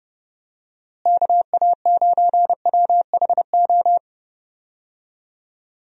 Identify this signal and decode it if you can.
Morse code, heard as a keyed tone, about 30 wpm, 700 Hz. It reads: KA9WHO
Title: USA Call Signs - 6 Characters - Repeated 2x - 30wpm